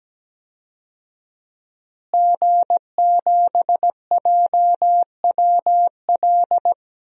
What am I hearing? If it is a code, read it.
Morse code, heard as a keyed tone, about 17 wpm, 700 Hz. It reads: G7JWL
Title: DXCC-100 Call Signs - 5 Characters - Repeated 3x - 17wpm